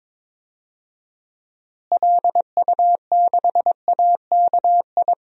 Morse code, heard as a keyed tone, about 22 wpm, 700 Hz. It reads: LU6AKI
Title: DXCC-100 Call Signs - 6 Characters - Repeated 3x - 22wpm